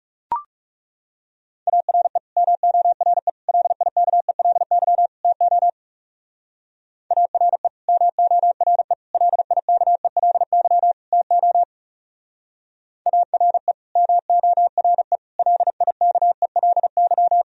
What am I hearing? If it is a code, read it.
Morse code, heard as a keyed tone, about 36 wpm, 700 Hz. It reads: ARE MORE LIKELY TO ARE MORE LIKELY TO ARE MORE LIKELY
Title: Speed-Racing - Sets of 4 Words 30wpm